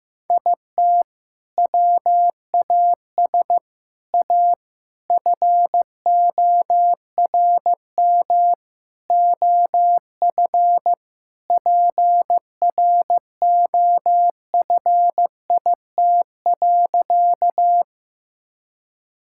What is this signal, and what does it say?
Morse code, heard as a keyed tone, about 15 wpm, 700 Hz. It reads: IT WAS A FORM OF PROFIT.
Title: Sentences from Top 1000 Words - Review Part 5 - 15wpm